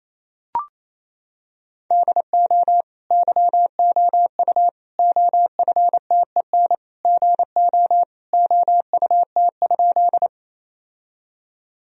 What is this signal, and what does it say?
Morse code, heard as a keyed tone, about 28 wpm, 700 Hz. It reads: DO YOU OFTEN GO OUT?
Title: Sentences from Top 1000 Words - Review Part 1 - 28wpm